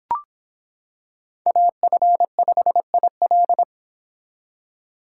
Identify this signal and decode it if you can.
Morse code, heard as a keyed tone, about 26 wpm, 700 Hz. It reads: AF5IL